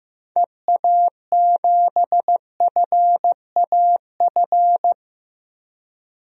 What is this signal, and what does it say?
Morse code, heard as a keyed tone, about 15 wpm, 700 Hz. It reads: EA7FAF
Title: DXCC-100 Call Signs - 6 Characters - Repeated 3x - 15wpm